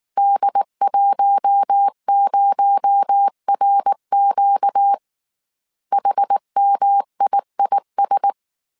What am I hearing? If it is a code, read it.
Morse code, heard as a keyed tone, about 19 wpm, 800 Hz. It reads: D10RQ HMIIS